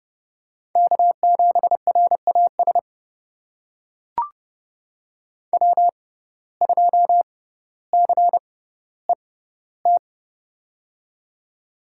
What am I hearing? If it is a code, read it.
Morse code, heard as a keyed tone, about 30 wpm, spaced as for 15 wpm, 700 Hz. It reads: K7RAS W2CET
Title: USA Call Signs - 6x Character Spacing - 30wpm